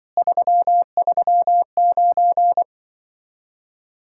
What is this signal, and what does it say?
Morse code, heard as a keyed tone, about 25 wpm, 700 Hz. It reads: 339